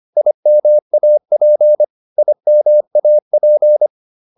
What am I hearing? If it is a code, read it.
Morse code, heard as a keyed tone, about 25 wpm, 600 Hz. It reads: IMAP IMAP